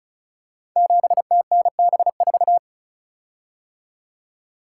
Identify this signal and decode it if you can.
Morse code, heard as a keyed tone, about 36 wpm, 700 Hz. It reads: 7TNB4
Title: Code Groups - 5 Letter:Numbers - 35wpm